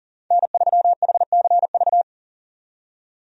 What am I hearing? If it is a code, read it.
Morse code, heard as a keyed tone, about 40 wpm, 700 Hz. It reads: N3HCV
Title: USA Call Signs - 5 Characters - 40wpm